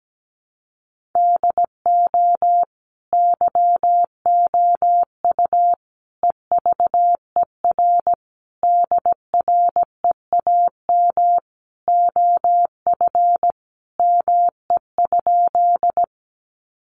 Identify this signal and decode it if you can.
Morse code, heard as a keyed tone, about 17 wpm, 700 Hz. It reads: DO YOU EVER DREAM OF ME?
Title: Sentences from Top 1500 Words 17wpm